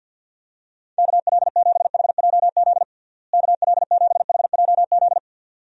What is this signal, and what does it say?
Morse code, heard as a keyed tone, about 52 wpm, 700 Hz. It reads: KL7HJZ KL7HJZ
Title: DXCC-100 Call Signs - 6 Characters - Repeated 2x - 50wpm